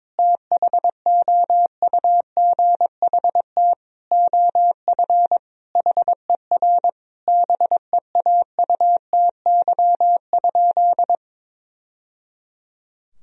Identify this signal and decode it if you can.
Morse code, heard as a keyed tone, about 22 wpm, 700 Hz. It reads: THOUGHT OF HER BEAUTY?
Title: Sentences from Top 1000 Words - Review Part 2 - 22wpm